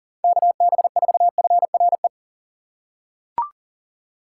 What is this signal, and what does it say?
Morse code, heard as a keyed tone, about 39 wpm, 700 Hz. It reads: KB4FRE
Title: USA Call Signs - 6 Characters - 3x Character Spacing - 40wpm